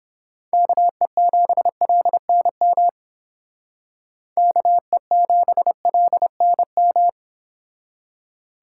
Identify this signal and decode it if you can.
Morse code, heard as a keyed tone, about 29 wpm, 700 Hz. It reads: KE7LNM KE7LNM